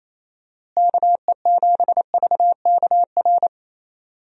Morse code, heard as a keyed tone, about 28 wpm, 700 Hz. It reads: KE7VKR
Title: USA Call Signs - 6 Characters - Repeated 3x - 28wpm